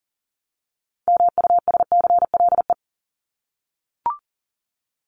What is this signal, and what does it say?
Morse code, heard as a keyed tone, about 40 wpm, 700 Hz. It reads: MUSCLE
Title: Top 1251-1500 Words 40wpm